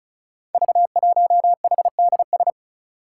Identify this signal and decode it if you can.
Morse code, heard as a keyed tone, about 36 wpm, 700 Hz. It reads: V1HDS
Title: Five Challenging Letter-Number Combinations 35wpm